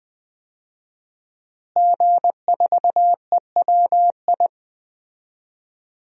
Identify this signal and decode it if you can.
Morse code, heard as a keyed tone, about 20 wpm, 700 Hz. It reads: G4EWI
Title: DXCC-100 Call Signs - 5 Characters - Repeated 2x - 20wpm